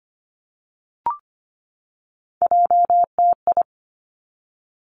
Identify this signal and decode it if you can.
Morse code, heard as a keyed tone, about 25 wpm, 700 Hz. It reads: JTI
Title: Three Challenging Letter-Number Combinations 25wpm